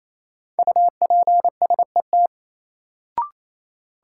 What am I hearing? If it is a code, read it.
Morse code, heard as a keyed tone, about 28 wpm, 700 Hz. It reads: UPSET
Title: Sentences from Top 800 Words - New Word Drill - 28wpm